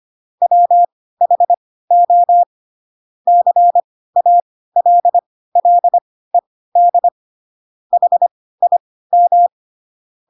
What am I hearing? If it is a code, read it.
Morse code, heard as a keyed tone, about 25 wpm, spaced as for 16 wpm, 700 Hz. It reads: WHO CALLED HIM
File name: Pinocchio / Pinocchio ch2 10 25 ik1hge